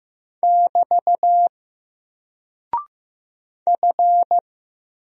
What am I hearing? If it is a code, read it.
Morse code, heard as a keyed tone, about 15 wpm, 700 Hz. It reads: <BT> F
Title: Single Character (Letter:Number:Punctuation:Prosign) 15wpm